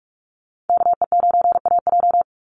Morse code, heard as a keyed tone, about 44 wpm, 700 Hz. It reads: KE9AJ
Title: NA SOTA Chaser Callsigns 45wpm